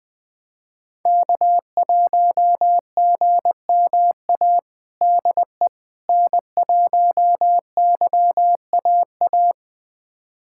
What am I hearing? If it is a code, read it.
Morse code, heard as a keyed tone, about 20 wpm, 700 Hz. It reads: K1GMA DE N1YAA
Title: USA Callsign DE Callsign - 5 Characters - 20wpm